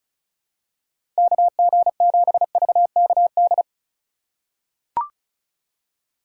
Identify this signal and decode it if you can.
Morse code, heard as a keyed tone, about 35 wpm, 700 Hz. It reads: KG7VKD
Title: USA Call Signs - 6 Characters - 3x Character Spacing - 35wpm